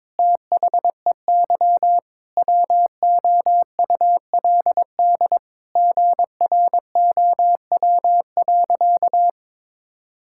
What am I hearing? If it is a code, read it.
Morse code, heard as a keyed tone, about 22 wpm, 700 Hz. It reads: THEY WOULD GROW.
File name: Sentences with 9-Word Phrases - 22wpm